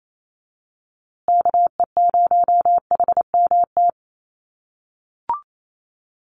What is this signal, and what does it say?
Morse code, heard as a keyed tone, about 28 wpm, 700 Hz. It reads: KE0HMT
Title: USA Call Signs Repeated 2x 28wpm